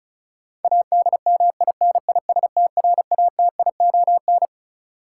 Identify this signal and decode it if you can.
Morse code, heard as a keyed tone, about 34 wpm, 700 Hz. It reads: ADMINISTRATION
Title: Sentences from Top 1300 Words - New Word Drill Alt - 35wpm